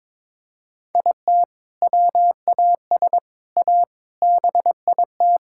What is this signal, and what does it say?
Morse code, heard as a keyed tone, about 22 wpm, 700 Hz. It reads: IT WAS A BIT
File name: practice / 22wpm / Sets of 6 Words - 22wpm